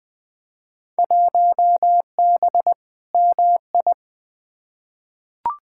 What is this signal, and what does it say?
Morse code, heard as a keyed tone, about 20 wpm, 700 Hz. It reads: 1B MI
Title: Field Day Exchanges - 20wpm